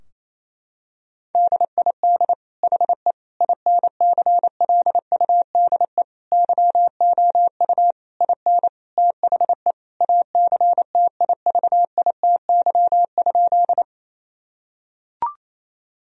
Morse code, heard as a keyed tone, about 28 wpm, 700 Hz. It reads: DID HE INCLUDE YOU IN THE ACTIVITY?
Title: Sentences from Top 1000 Words - Review Part 4 - 28wpm